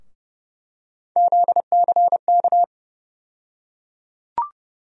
Morse code, heard as a keyed tone, about 30 wpm, 700 Hz. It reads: ZCK
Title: USA Call Sign Postfixes - 3 Letters 30wpm